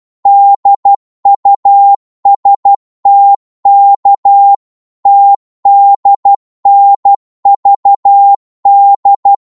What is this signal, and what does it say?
Morse code, heard as a keyed tone, about 12 wpm, 800 Hz. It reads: DUSTK TDNVD
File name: Morse / ltr-12wpm